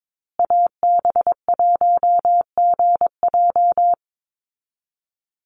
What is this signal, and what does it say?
Morse code, heard as a keyed tone, about 22 wpm, 690 Hz. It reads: AB1GJ
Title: Call Signs Repeated 3x - 22wpm